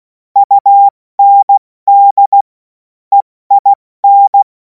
Morse code, heard as a keyed tone, about 16 wpm, spaced as for 14 wpm, 800 Hz. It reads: UND EIN
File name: Morsekurs / Witze / Witz 14. April 2026 70 F80